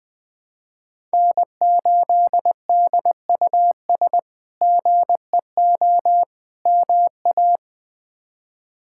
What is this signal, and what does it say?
Morse code, heard as a keyed tone, about 20 wpm, 700 Hz. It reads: N8DUS GEO MA
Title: K1USN SST QSO Practice - 20wpm